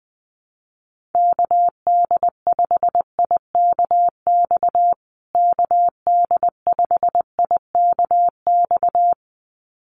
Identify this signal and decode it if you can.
Morse code, heard as a keyed tone, about 19 wpm, 700 Hz. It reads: KD5IKX KD5IKX
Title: USA Call Signs Repeated 2x 20wpm